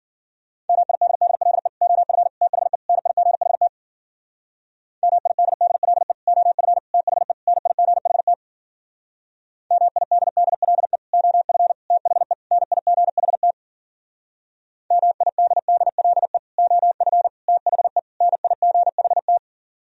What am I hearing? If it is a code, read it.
Morse code, heard as a keyed tone, about 49 wpm, 700 Hz. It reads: MIDDLE OF THE NIGHT MIDDLE OF THE NIGHT MIDDLE OF THE NIGHT MIDDLE OF THE NIGHT
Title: Speed-Racing - Sets of 4 Words 40wpm